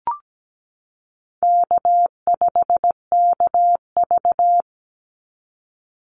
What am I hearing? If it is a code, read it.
Morse code, heard as a keyed tone, about 17 wpm, 700 Hz. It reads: K5KV
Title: Easy Call Signs - 17wpm